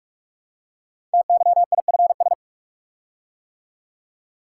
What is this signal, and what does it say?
Morse code, heard as a keyed tone, about 44 wpm, 700 Hz. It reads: TYIFS
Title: Code Groups - 5 Letter:Numbers - 45wpm